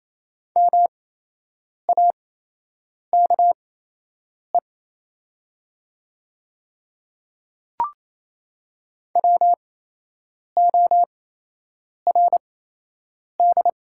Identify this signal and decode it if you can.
Morse code, heard as a keyed tone, about 28 wpm, spaced as for 8 wpm, 700 Hz. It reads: MAKE WORD
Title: 8x Character Spacing - Top 100 Words - 28wpm